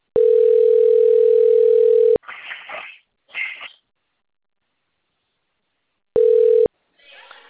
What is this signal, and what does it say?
Ringback tone (North America), 440 + 480 Hz, with speech.
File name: out- - - - - .37440